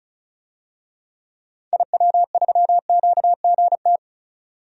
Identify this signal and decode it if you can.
Morse code, heard as a keyed tone, about 36 wpm, 700 Hz. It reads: IW3QGT
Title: DXCC-100 Call Signs - 6 Characters - 35wpm